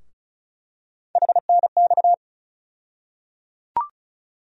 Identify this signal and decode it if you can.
Morse code, heard as a keyed tone, about 36 wpm, 700 Hz. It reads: HNX
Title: USA Call Sign Postfixes - 3 Letters 35wpm